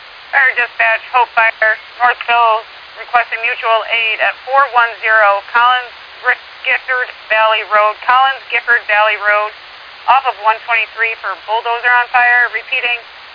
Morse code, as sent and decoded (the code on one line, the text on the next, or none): none